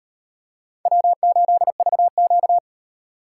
Morse code, heard as a keyed tone, about 38 wpm, 700 Hz. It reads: W8VQ